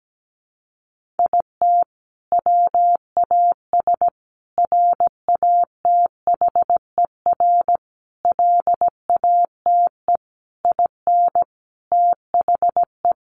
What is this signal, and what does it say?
Morse code, heard as a keyed tone, about 17 wpm, 700 Hz. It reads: IT WAS RATHER LATE IN THE
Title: Sentences from Top 1400 Words 17wpm